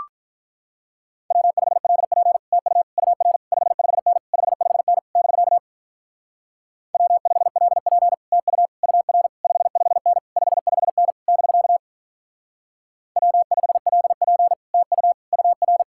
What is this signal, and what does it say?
Morse code, heard as a keyed tone, about 45 wpm, 700 Hz. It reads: W5LP TU UR 55N 55N <BK> W5LP TU UR 55N 55N <BK> W5LP TU UR